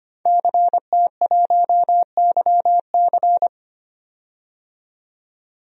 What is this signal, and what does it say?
Morse code, heard as a keyed tone, about 25 wpm, 700 Hz. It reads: CT1YC
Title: DXCC-100 Call Signs - 5 Characters - 25wpm